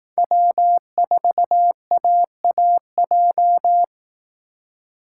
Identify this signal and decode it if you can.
Morse code, heard as a keyed tone, about 18 wpm, 700 Hz. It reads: W4AAJ
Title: Speed Racing - USA 1x3 Call Signs 15wpm